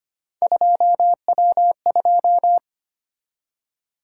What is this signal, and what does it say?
Morse code, heard as a keyed tone, about 25 wpm, 700 Hz. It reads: 2W2
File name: Three Letter-Number Combinations - 25wpm